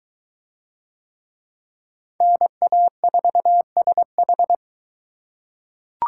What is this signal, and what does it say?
Morse code, heard as a keyed tone, about 22 wpm, 700 Hz. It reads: NA4SH